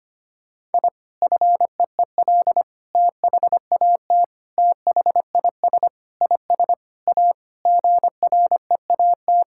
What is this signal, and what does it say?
Morse code, heard as a keyed tone, about 25 wpm, 700 Hz. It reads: I FEEL THAT THIS IS A GREAT
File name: practice / 25wpm / Sentences with 4-Word Phrases - 25wpm